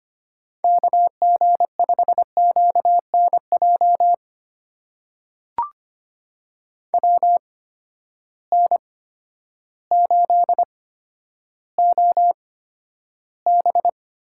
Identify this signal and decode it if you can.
Morse code, heard as a keyed tone, about 25 wpm, spaced as for 11 wpm, 700 Hz. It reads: KG5QNJ WN8OB
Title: USA Call Signs - 8x Character Spacing - 25wpm